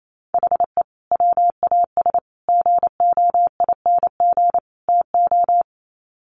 Morse code, heard as a keyed tone, about 28 wpm, 700 Hz. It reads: HE WAS GOING TO
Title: Sets of 4 Words 28wpm